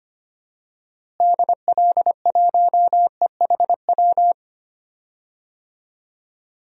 Morse code, heard as a keyed tone, about 25 wpm, 700 Hz. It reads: DL1EHW
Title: DXCC-100 Call Signs - 6 Characters - Repeated 2x - 25wpm